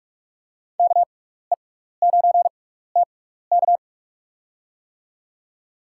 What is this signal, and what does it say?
Morse code, heard as a keyed tone, about 45 wpm, spaced as for 15 wpm, 700 Hz. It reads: KE9TK
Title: USA Call Signs - 5 Characters - 6x Character Spacing - 45wpm